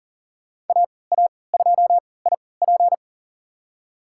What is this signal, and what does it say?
Morse code, heard as a keyed tone, about 40 wpm, spaced as for 23 wpm, 700 Hz. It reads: AA2IP
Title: USA Call Signs - 5 Characters - 3x Character Spacing - 40wpm